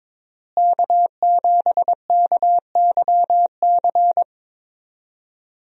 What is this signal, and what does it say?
Morse code, heard as a keyed tone, about 22 wpm, 700 Hz. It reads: K7KYC
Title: USA Call Signs - 5 Characters - Repeated 2x - 22wpm